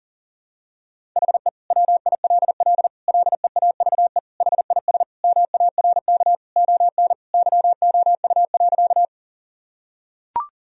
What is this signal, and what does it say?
Morse code, heard as a keyed tone, about 40 wpm, 700 Hz. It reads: HE WILL LEAVE HIS MARK ON YOU.
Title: Sentences from Top 1000 Words - Review Part 1 - 40wpm